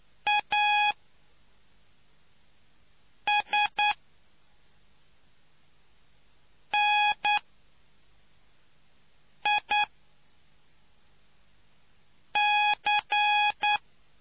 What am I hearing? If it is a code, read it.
Morse code, heard as a keyed tone, about 9 wpm, spaced as for 3 wpm, 830 Hz. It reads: ASNIC